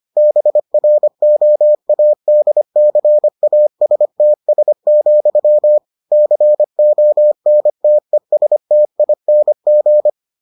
Morse code, heard as a keyed tone, about 25 wpm, 600 Hz. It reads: BROADCASTS, CONTESTING